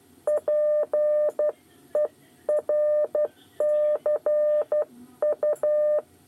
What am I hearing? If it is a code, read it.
Morse code, heard as a keyed tone, about 12 wpm, 600 Hz. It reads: PERCU
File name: 8 bis lezione cw 2